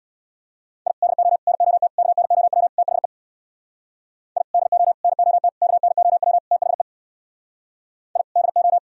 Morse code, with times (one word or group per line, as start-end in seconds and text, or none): 0.9-0.9 I
1.0-1.4 DO
1.5-1.9 NOT
2.0-2.7 KNOW
2.8-3.1 THE
4.4-4.4 I
4.5-4.9 DO
5.0-5.5 NOT
5.6-6.4 KNOW
6.5-6.8 THE
8.2-8.2 I
8.4-8.8 DO